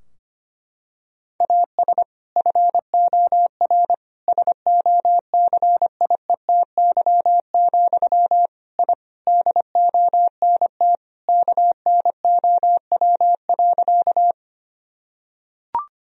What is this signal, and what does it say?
Morse code, heard as a keyed tone, about 25 wpm, 700 Hz. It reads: AS FOR SOCIETY, I DONT KNOW.
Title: Sentences from Top 1000 Words - Review Part 4 - 25wpm